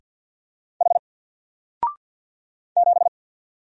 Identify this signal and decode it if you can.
Morse code, heard as a keyed tone, about 50 wpm, 700 Hz. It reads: H 7
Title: Single Letter-Number 50wpm